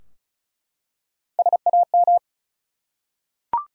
Morse code, heard as a keyed tone, about 36 wpm, 700 Hz. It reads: SAM
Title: USA Common Names - Max 5 Letters - 35wpm